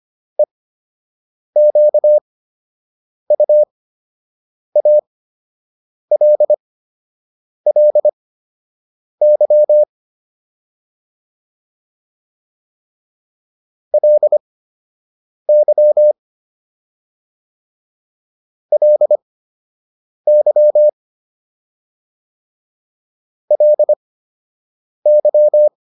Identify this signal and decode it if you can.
Morse code, heard as a keyed tone, about 25 wpm, spaced as for 7 wpm, 600 Hz. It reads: EQUALLY LY LY LY